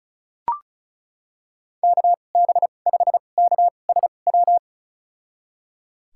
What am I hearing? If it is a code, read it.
Morse code, heard as a keyed tone, about 34 wpm, spaced as for 25 wpm, 700 Hz. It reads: KB5KSW
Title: USA Call Signs - 2x Character Spacing - 35wpm